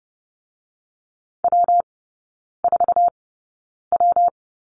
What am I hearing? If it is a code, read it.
Morse code, heard as a keyed tone, about 30 wpm, spaced as for 9 wpm, 700 Hz. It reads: W4W
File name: Lesson 19e - Letter w - Three Characters 30wpm